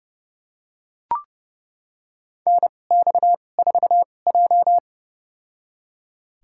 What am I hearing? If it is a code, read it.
Morse code, heard as a keyed tone, about 32 wpm, spaced as for 22 wpm, 700 Hz. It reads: NX4J
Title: USA Call Signs - 4 Characters - 2x Character Spacing - 30wpm